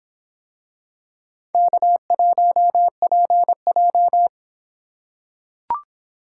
Morse code, heard as a keyed tone, about 25 wpm, 700 Hz. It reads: K1PJ